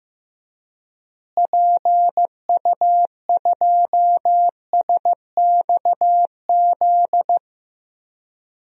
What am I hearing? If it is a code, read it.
Morse code, heard as a keyed tone, about 15 wpm, 700 Hz. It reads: PU2SXZ
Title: DXCC-100 Call Signs - 6 Characters - Repeated 2x - 15wpm